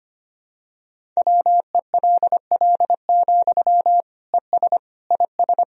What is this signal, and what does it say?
Morse code, heard as a keyed tone, about 25 wpm, 700 Hz. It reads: WELL, ES IS
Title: Sentences from Top 200 Words and Abbreviations 25wpm